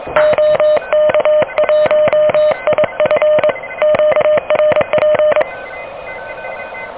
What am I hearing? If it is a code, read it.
Morse code, heard as a keyed tone, about 23 wpm, 600 Hz, with noise. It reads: OK1IF QRP
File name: OK QRP 2023 / OK QRP 2023